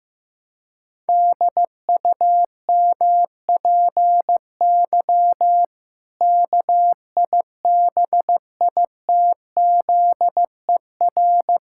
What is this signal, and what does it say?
Morse code, heard as a keyed tone, about 16 wpm, 700 Hz. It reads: DUMPY KIBITZER
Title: Pangram Challenge v2 - 15wpm